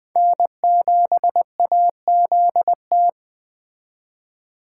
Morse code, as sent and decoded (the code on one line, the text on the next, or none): -. --... .- --.. -
N7AZT